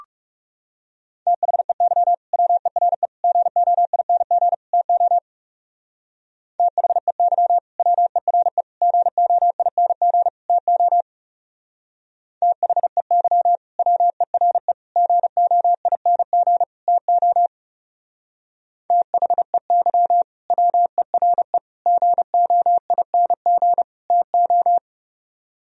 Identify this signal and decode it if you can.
Morse code, heard as a keyed tone, about 37 wpm, 700 Hz. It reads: THEY WERE GOING TO THEY WERE GOING TO THEY WERE GOING TO THEY WERE GOING TO
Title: Speed-Racing - Sets of 4 Words 30wpm